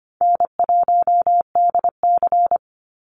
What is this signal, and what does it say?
Morse code, heard as a keyed tone, about 25 wpm, 700 Hz. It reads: N1DC
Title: Easy Call Signs - 25wpm